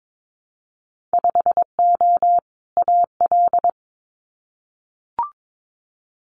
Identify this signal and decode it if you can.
Morse code, heard as a keyed tone, about 22 wpm, 700 Hz. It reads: 5O AL
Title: Winter Field Day Exchanges 22wpm